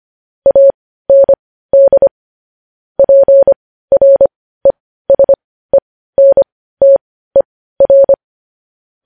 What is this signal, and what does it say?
Morse code, heard as a keyed tone, about 25 wpm, spaced as for 15 wpm, 550 Hz. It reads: AND PRESENTER
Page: news-headlines-in-morse-code-at-15-wpm